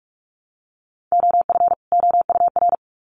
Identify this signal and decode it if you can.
Morse code, heard as a keyed tone, about 44 wpm, 700 Hz. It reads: OF OUR
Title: Sets of 2 Words 45wpm